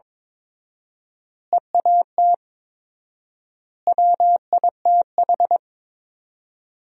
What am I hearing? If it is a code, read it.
Morse code, heard as a keyed tone, about 22 wpm, 700 Hz. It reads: EAT WITH
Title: 4x Word-Spacing - Sentences from Top 300 Words - 22wpm